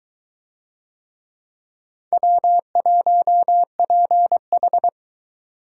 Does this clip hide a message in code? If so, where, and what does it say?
Yes: Morse code from 2.1 to 4.9 s, heard as a keyed tone, about 20 wpm, 700 Hz. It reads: W1PH